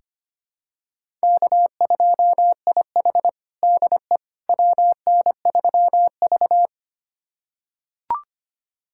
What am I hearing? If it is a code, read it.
Morse code, heard as a keyed tone, about 25 wpm, 700 Hz. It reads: K2IH DE WN3V
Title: USA Callsign DE Callsign - 4 Characters - 25wpm